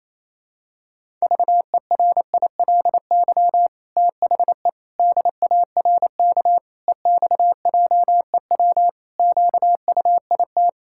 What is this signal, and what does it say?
Morse code, heard as a keyed tone, about 28 wpm, 700 Hz. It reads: VERILY THE DARK EXJEW QUIT
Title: Pangram Challenge v2 - 28wpm